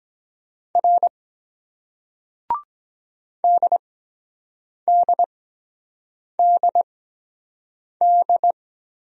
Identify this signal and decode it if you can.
Morse code, heard as a keyed tone, about 22 wpm, spaced as for 7 wpm, 700 Hz. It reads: R DDDD